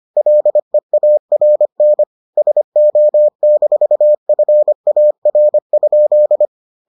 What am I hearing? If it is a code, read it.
Morse code, heard as a keyed tone, about 25 wpm, 600 Hz. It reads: LEARN SO-FAR?